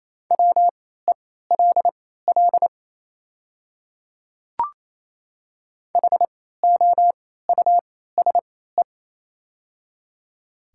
Morse code, heard as a keyed tone, about 28 wpm, spaced as for 16 wpm, 700 Hz. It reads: WELL HOUSE
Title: 3x Character Spacing - Top 300 Words - 28wpm